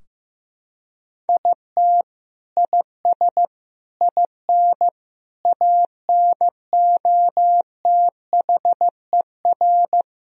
Morse code, heard as a keyed tone, about 15 wpm, 700 Hz. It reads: IT IS IN ANOTHER
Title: Sentences from Top 1000 Words - Review Part 2 - 15wpm